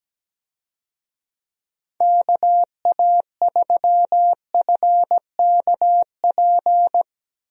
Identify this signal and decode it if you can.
Morse code, heard as a keyed tone, about 17 wpm, 700 Hz. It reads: KA3FKP
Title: USA Call Signs - 6 Characters - Repeated 3x - 17wpm